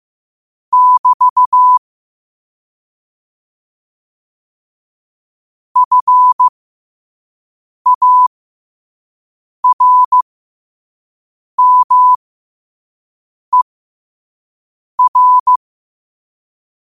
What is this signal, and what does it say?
Morse code, heard as a keyed tone, about 15 wpm, spaced as for 5 wpm, 1000 Hz. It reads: <BT> FARMER
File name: qotdfiles / 5wpm